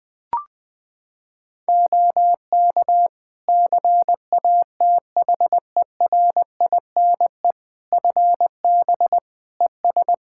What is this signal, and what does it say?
Morse code, heard as a keyed tone, about 21 wpm, 700 Hz. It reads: OK CATHERINE FB ES
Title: Practice QSO - Exchange 2 - Info - 20wpm